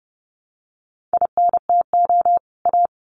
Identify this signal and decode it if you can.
Morse code, heard as a keyed tone, about 30 wpm, 700 Hz. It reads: INTO A
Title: Sets of 2 Words 30wpm